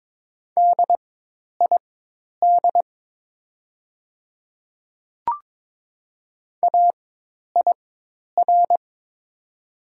Morse code, heard as a keyed tone, about 22 wpm, spaced as for 10 wpm, 700 Hz. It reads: DID AIR
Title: 4x Character Spacing - Top 300 Words - 22wpm